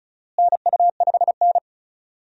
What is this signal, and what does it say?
Morse code, heard as a keyed tone, about 36 wpm, 700 Hz. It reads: NU5N
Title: USA Call Signs - 4 Characters - Repeated 3x - 35wpm